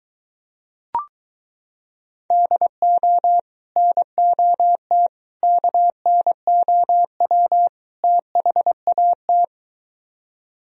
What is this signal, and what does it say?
Morse code, heard as a keyed tone, about 23 wpm, 700 Hz. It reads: DO NOT KNOW THAT